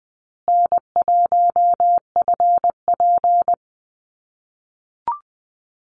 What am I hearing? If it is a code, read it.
Morse code, heard as a keyed tone, about 20 wpm, 700 Hz. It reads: N1FP
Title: USA Call Signs Repeated 3x 20wpm